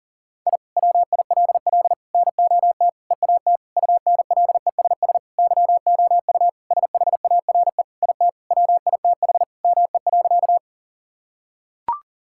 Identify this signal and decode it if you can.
Morse code, heard as a keyed tone, about 37 wpm, 700 Hz. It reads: I WILL NOT EAT UNLESS YOU SHARE IT WITH ME.
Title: Sentences from Top 1000 Words - Review Part 6 - 40wpm